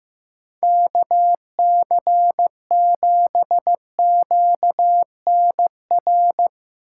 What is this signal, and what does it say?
Morse code, heard as a keyed tone, about 16 wpm, 700 Hz. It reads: KC7QNR
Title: USA Call Signs - 6 Characters - 2x Character Spacing - 15wpm